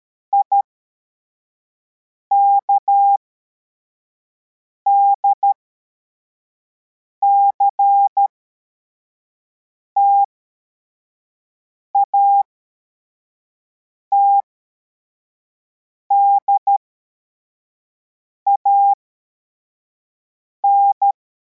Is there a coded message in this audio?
Yes: Morse code from 0.3 to 21.1 s, heard as a keyed tone, about 13 wpm, spaced as for 4 wpm, 800 Hz. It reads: IKDCT ATDAN